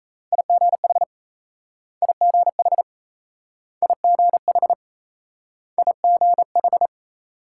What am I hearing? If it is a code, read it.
Morse code, heard as a keyed tone, about 38 wpm, 700 Hz. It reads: IGH IGH IGH IGH